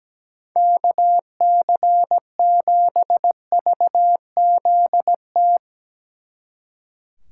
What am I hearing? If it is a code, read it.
Morse code, heard as a keyed tone, about 18 wpm, 700 Hz. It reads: KC7VZT
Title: Speed Racing - USA 2x3 Call Signs 17wpm